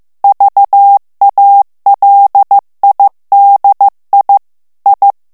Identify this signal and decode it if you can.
Morse code, heard as a keyed tone, about 15 wpm, 800 Hz. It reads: VALIDI I
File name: morse mp3 / ITALIEN texte clair 2 vitesse 14 mots minute